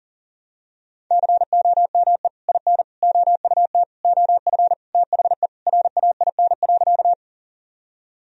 Morse code, heard as a keyed tone, about 40 wpm, 700 Hz. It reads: COME IN OUT OF THE RAIN.
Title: Sentences from Top 1000 Words - Review Part 1 - 40wpm